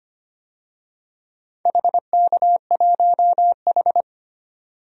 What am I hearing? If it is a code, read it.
Morse code, heard as a keyed tone, about 25 wpm, 700 Hz. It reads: HK1H
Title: DXCC-100 Call Signs - 4 Characters - 25wpm